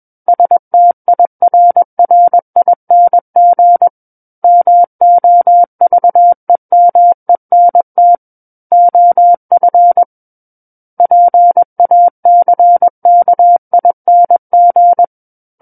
Morse code, heard as a keyed tone, about 21 wpm, 700 Hz. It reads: STIRRING MOVEMENT OF PACKING
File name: Book One / 12-Chapter 12 - 21 WPM